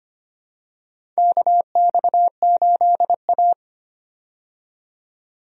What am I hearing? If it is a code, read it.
Morse code, heard as a keyed tone, about 25 wpm, 700 Hz. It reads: KX8A
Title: USA Call Signs - 4 Characters - 25wpm